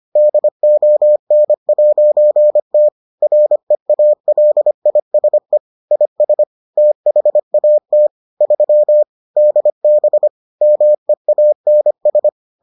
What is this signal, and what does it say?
Morse code, heard as a keyed tone, about 25 wpm, 600 Hz. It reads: DON'T REALISE IS THAT 3 DB MEANS